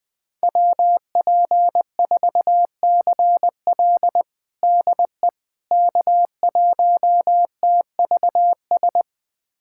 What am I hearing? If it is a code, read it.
Morse code, heard as a keyed tone, about 20 wpm, 700 Hz. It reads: WP4CL DE K1TVS
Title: USA Callsign DE Callsign - 5 Characters - 20wpm